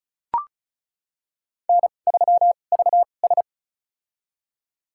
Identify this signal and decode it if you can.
Morse code, heard as a keyed tone, about 36 wpm, spaced as for 25 wpm, 700 Hz. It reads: N3VS
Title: USA Call Signs - 2x Character Spacing - 35wpm